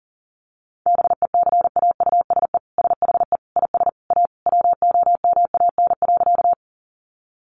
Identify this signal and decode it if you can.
Morse code, heard as a keyed tone, about 40 wpm, 700 Hz. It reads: BECAUSE SHE IS A WOMAN.
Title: Sentences from Top 1100 Words 40wpm